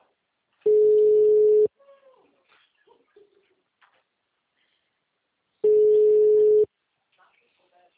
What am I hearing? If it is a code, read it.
Ringback tone (Europe, 425 Hz), with speech.